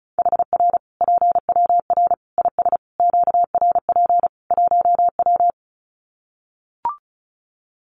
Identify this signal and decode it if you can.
Morse code, heard as a keyed tone, about 34 wpm, 700 Hz. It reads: HR PWR IS QRP 1W
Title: Common QSO Phrases 35wpm